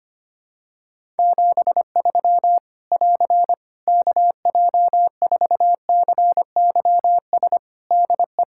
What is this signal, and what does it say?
Morse code, heard as a keyed tone, about 25 wpm, 700 Hz. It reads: 73 <AR> KJ4CYS DE
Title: Practice QSO - Exchange 4 - Ending - 25wpm